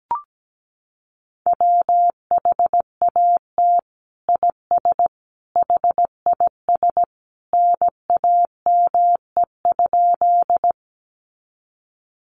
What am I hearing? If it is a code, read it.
Morse code, heard as a keyed tone, about 17 wpm, 700 Hz. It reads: WHAT IS HIS NAME?
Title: Sentences from Top 200 Words 17wpm